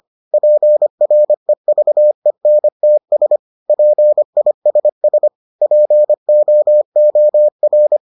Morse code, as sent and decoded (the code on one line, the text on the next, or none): .--. .-. ...- . -. - ... / .--. .. ... ... / .--. --- --- .-.
PREVENTS PISS POOR